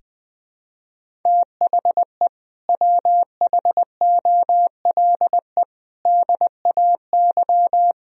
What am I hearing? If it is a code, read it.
Morse code, heard as a keyed tone, about 20 wpm, 700 Hz. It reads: THE WHOLE DAY
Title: Sentences from Top 500 Words and Abbreviations 20wpm